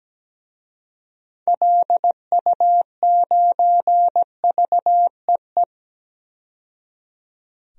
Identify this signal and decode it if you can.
Morse code, heard as a keyed tone, about 17 wpm, 700 Hz. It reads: LU9VEE
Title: DXCC-100 Call Signs - 6 Characters - 17wpm